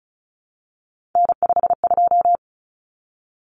Morse code, heard as a keyed tone, about 34 wpm, 700 Hz. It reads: N52
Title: Common Cut Numbers - 3-Digit 35wpm